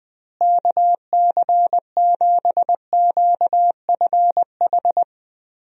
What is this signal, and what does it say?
Morse code, heard as a keyed tone, about 20 wpm, 700 Hz. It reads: KC7QFH
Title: USA Call Signs - 6 Characters - Repeated 3x - 20wpm